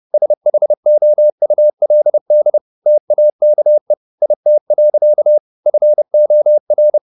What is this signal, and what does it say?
Morse code, heard as a keyed tone, about 30 wpm, 600 Hz. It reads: SHOULD TAKE IT. FOR
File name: the old man and the sea / the old man and the sea 1 30wpm